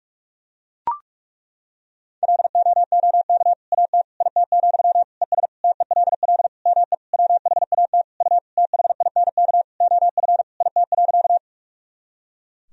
Morse code, heard as a keyed tone, about 41 wpm, 700 Hz. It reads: LOOK AT IT, ES TELL ME WHAT U THINK OF IT.
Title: Sentences from Top 200 Words and Abbreviations 45wpm